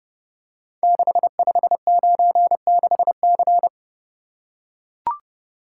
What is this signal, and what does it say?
Morse code, heard as a keyed tone, about 30 wpm, 700 Hz. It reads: 6596C
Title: Code Groups - 5 Letter:Numbers - 30wpm